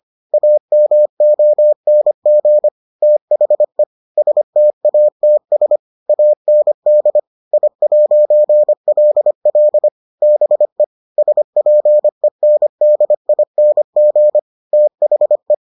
Morse code, heard as a keyed tone, about 25 wpm, 600 Hz. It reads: AMONG THE STATS AND I'LL BE SPENDING THE